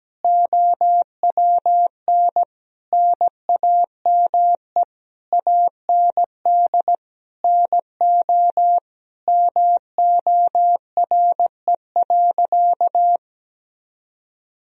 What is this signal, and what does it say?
Morse code, heard as a keyed tone, about 17 wpm, 700 Hz. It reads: OWN NAME AND NO MORE.
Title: Sentences from Top 1000 Words - Review Part 1 - 17wpm